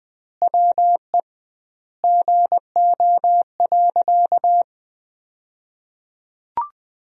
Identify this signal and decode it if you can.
Morse code, heard as a keyed tone, about 20 wpm, 700 Hz. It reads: WE GO.
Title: 2x Word-Spacing - Sentences from Top 200 Words - 20wpm